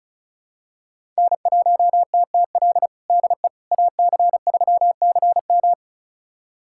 Morse code, heard as a keyed tone, about 36 wpm, 700 Hz. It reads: N1TTL DE AC3CM
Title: USA Callsign DE Callsign - 5 Characters - 35wpm